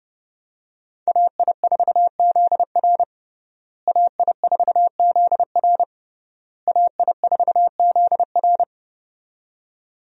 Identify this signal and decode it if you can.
Morse code, heard as a keyed tone, about 32 wpm, 700 Hz. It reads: AI4ZR AI4ZR AI4ZR
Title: USA Call Signs - 5 Characters - Repeated 3x - 30wpm